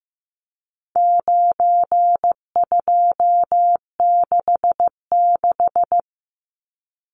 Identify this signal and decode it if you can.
Morse code, heard as a keyed tone, about 15 wpm, 700 Hz. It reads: 9266
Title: Numbers - 4-Digit 15wpm